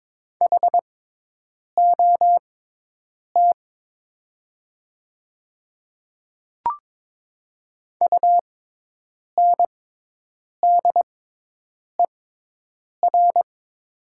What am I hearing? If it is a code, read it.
Morse code, heard as a keyed tone, about 22 wpm, spaced as for 8 wpm, 700 Hz. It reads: HOT UNDER
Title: 6x Character Spacing - Top 200 Words - 22wpm